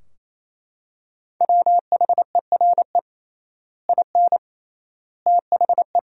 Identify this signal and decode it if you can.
Morse code, heard as a keyed tone, about 27 wpm, 700 Hz. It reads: WHERE IN THE
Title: 3x Word-Spacing - Sentences from Top 200 Words - 28wpm